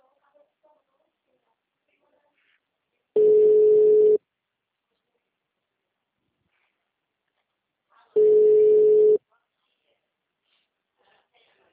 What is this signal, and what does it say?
Ringback tone (Europe, 425 Hz), with speech.